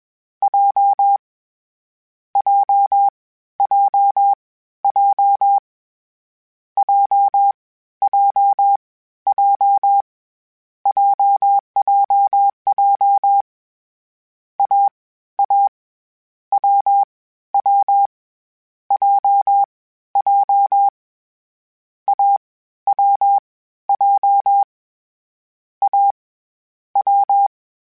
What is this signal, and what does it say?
Morse code, heard as a keyed tone, about 21 wpm, 800 Hz. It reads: J J J J J J J JJJ A A W W J J A W J A W
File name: CW Code Groups-mp3s / Sets mp3s / AUV sp @20